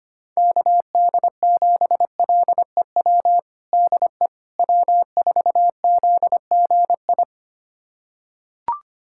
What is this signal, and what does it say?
Morse code, heard as a keyed tone, about 25 wpm, 700 Hz. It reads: KD7LEW DE W4ZGI
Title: USA Callsign DE Callsign 25wpm